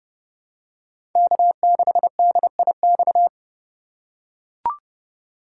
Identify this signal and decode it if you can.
Morse code, heard as a keyed tone, about 30 wpm, 700 Hz. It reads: K6DIX
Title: USA Call Signs - 3x Character Spacing - 30wpm